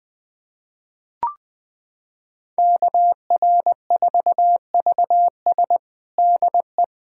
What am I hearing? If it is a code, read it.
Morse code, heard as a keyed tone, about 21 wpm, 700 Hz. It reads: KR4VS DE
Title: USA Callsign DE Callsign - 5 Characters - 20wpm